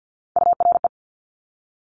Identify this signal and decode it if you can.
Morse code, heard as a keyed tone, about 50 wpm, 700 Hz. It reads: URE
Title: Top Three Letter Combinations in Words 50wpm